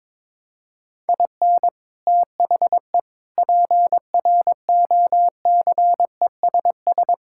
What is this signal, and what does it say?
Morse code, heard as a keyed tone, about 22 wpm, 700 Hz. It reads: IN THE PROCESS
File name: practice / 22wpm / Sentences with 3-Word Phrases - 22wpm